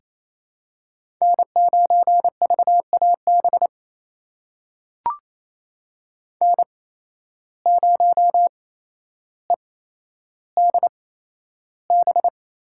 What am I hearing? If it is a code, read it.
Morse code, heard as a keyed tone, about 28 wpm, spaced as for 12 wpm, 700 Hz. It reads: N9VAB N0EDB
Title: USA Call Signs - 5 Characters - 8x Character Spacing - 28wpm